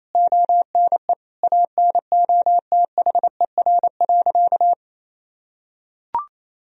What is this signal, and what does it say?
Morse code, heard as a keyed tone, about 28 wpm, 700 Hz. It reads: ONE ANOTHER.
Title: Sentences from Top 1000 Words - Review Part 1 - 28wpm